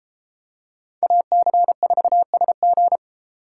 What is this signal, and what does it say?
Morse code, heard as a keyed tone, about 32 wpm, 700 Hz. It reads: AC4SG